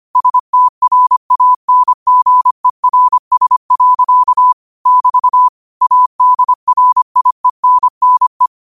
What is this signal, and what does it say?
Morse code, heard as a keyed tone, about 25 wpm, 1000 Hz. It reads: STRANGERS. <BT> ADRIENNE